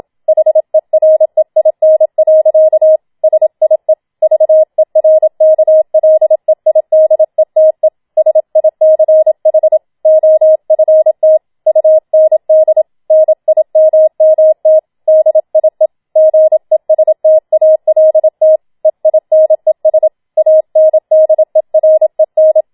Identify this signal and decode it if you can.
Morse code, heard as a keyed tone, about 27 wpm, 630 Hz. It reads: HEREIN. SIE VERKLEIDETE SICH OFT UND NIMMT DIE GESTALT EINES ANDEREN
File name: Schneewittchen Teil2 27wpm